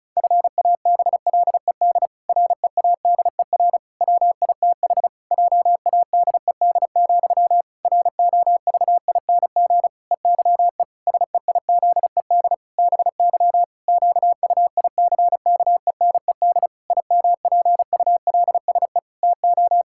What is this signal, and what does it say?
Morse code, heard as a keyed tone, about 36 wpm, 700 Hz. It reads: FABLED READER WITH JADED, ROVING EYE SEIZED BY QUICKENED IMPULSE TO
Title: Pangram Challenge v2 - 35wpm